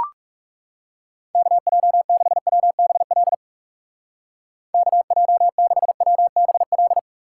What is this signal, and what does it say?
Morse code, heard as a keyed tone, about 37 wpm, 700 Hz. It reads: KJ6WBL KJ6WBL
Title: Speed Racing - USA Call Signs 30wpm